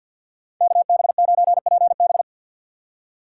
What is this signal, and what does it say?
Morse code, heard as a keyed tone, about 58 wpm, 700 Hz. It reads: KB9PB
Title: Speed Racing - USA 2x2 Call Signs 50wpm